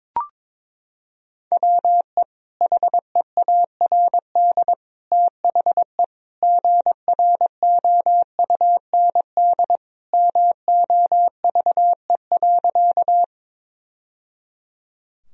Morse code, heard as a keyed tone, about 22 wpm, 700 Hz. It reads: WE HEARD THE GROUND MOVE.
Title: Sentences from Top 1000 Words - Review Part 2 - 22wpm